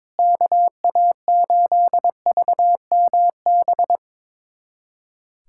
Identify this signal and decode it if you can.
Morse code, heard as a keyed tone, about 22 wpm, 700 Hz. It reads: KA8VMB
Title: USA Callsign DE Callsign 22wpm